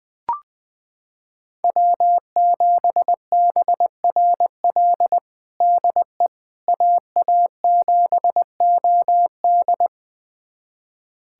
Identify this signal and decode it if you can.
Morse code, heard as a keyed tone, about 20 wpm, 700 Hz. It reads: W7BRL DE AA7OD
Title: USA Callsign DE Callsign - 5 Characters - 20wpm